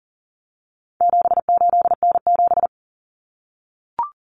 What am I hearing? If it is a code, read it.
Morse code, heard as a keyed tone, about 40 wpm, 700 Hz. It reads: 78N7
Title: Common Cut Numbers - 4-Digit 40wpm